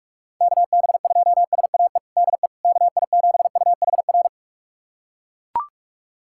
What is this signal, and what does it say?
Morse code, heard as a keyed tone, about 42 wpm, 700 Hz. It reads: KB2SAE DE KI7UHR
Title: USA Callsign DE Callsign - 6 Characters - 45wpm